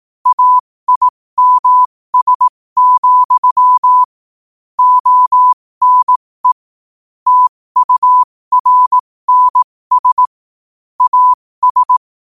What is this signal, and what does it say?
Morse code, heard as a keyed tone, about 18 wpm, spaced as for 15 wpm, 1000 Hz. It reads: AIMS, ONE TURNS AS
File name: qotdfiles / 15wpm